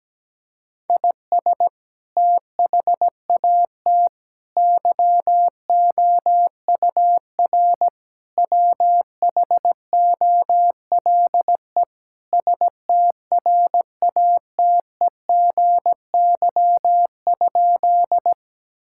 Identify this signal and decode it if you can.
Morse code, heard as a keyed tone, about 17 wpm, 700 Hz. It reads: IS THAT YOUR WHOLE STRATEGY?
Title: Sentences from Top 1000 Words - Review Part 5 - 17wpm